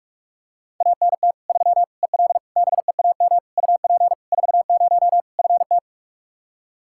Morse code, heard as a keyed tone, about 41 wpm, 700 Hz. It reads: ANT 3 EL BEAM UP 40 FT
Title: Practice QSO - Exchange 2 - Info - 45wpm